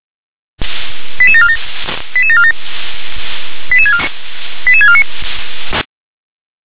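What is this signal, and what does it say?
Five-tone selective call (ZVEI), heard as quick runs of five tones, tones of about 70 ms. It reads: 88747 88747 88649
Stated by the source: ZVEI five-tone selective call 88915 received via SDR.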